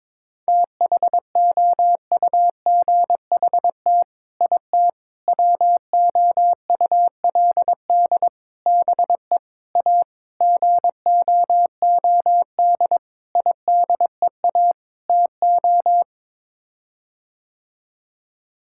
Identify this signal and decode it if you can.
Morse code, heard as a keyed tone, about 22 wpm, 700 Hz. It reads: THOUGHT IT WOULD BE A GOOD IDEA TO
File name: practice / 22wpm / Sets of 8 Words - 22wpm